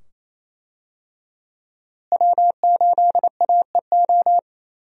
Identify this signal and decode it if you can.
Morse code, heard as a keyed tone, about 28 wpm, 700 Hz. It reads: W8AEO
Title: USA Call Signs - 5 Characters - Repeated 2x - 28wpm